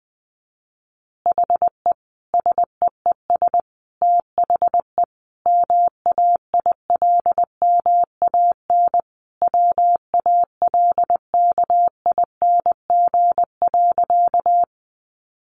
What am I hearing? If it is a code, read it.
Morse code, heard as a keyed tone, about 20 wpm, 700 Hz. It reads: HE SEES THE MAILMAN WALKING.
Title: Additional Easy Phrases 20wpm